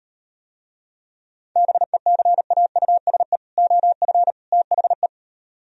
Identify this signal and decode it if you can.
Morse code, heard as a keyed tone, about 38 wpm, 700 Hz. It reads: BECAUSE OF THE